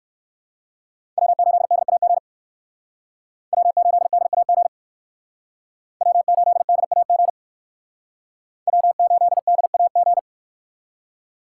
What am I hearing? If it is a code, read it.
Morse code, heard as a keyed tone, about 58 wpm, 700 Hz. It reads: W8DAG W8DAG W8DAG W8DAG